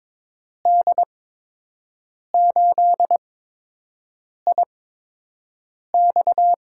Morse code, heard as a keyed tone, about 22 wpm, spaced as for 6 wpm, 700 Hz. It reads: D8IX
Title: USA Call Signs - 8x Character Spacing - 22wpm